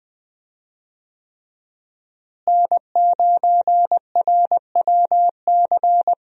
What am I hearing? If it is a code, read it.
Morse code, heard as a keyed tone, about 20 wpm, 700 Hz. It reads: N9RWC
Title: USA Call Signs - 5 Characters - Repeated 3x - 20wpm